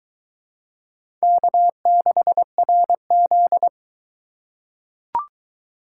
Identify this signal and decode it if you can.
Morse code, heard as a keyed tone, about 20 wpm, 700 Hz. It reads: K6RZ